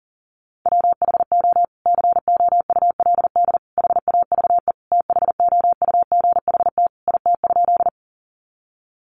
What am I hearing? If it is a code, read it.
Morse code, heard as a keyed tone, about 37 wpm, 700 Hz. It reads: WHO COULD HAVE THOUGHT IT?
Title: Sentences from Top 300 Words 40wpm